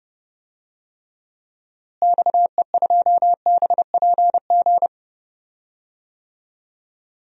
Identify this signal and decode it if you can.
Morse code, heard as a keyed tone, about 32 wpm, 700 Hz. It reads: XE2BPG
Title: DXCC-100 Call Signs - 6 Characters - Repeated 3x - 30wpm